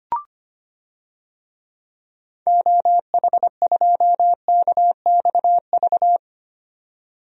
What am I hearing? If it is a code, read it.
Morse code, heard as a keyed tone, about 25 wpm, 700 Hz. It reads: OH2KXV
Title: DXCC-100 Call Signs - 6 Characters - 25wpm